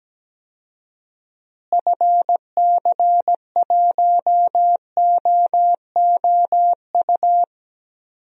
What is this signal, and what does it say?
Morse code, heard as a keyed tone, about 17 wpm, 700 Hz. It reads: FC1OOU
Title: DXCC-100 Call Signs - 6 Characters - Repeated 3x - 17wpm